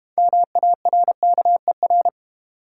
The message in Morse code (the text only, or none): MARKER